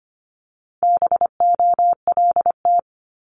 Morse code, heard as a keyed tone, about 25 wpm, 700 Hz. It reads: BOLT
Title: Multi-letter Phonemes 25wpm